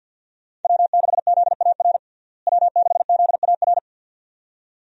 Morse code, heard as a keyed tone, about 50 wpm, 700 Hz. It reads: W6ZAR W6ZAR
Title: USA Call Signs - 5 Characters - Repeated 2x - 50wpm